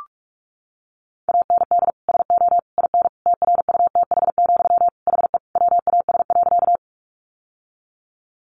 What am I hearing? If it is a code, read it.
Morse code, heard as a keyed tone, about 44 wpm, 700 Hz. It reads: AND SO IN TRUTH, HE WAS.
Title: Sentences from Top 800 Words 45wpm